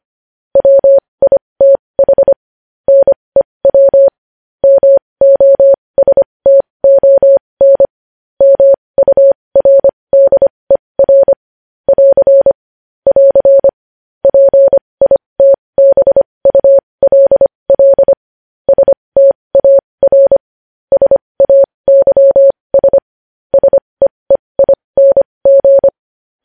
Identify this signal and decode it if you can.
Morse code, heard as a keyed tone, about 25 wpm, spaced as for 20 wpm, 550 Hz. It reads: WITH NEW MOSTON MURDER <AR> <AR> PITBULL STAR SAYS SEEING
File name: podcast / News headlines-1x-25x20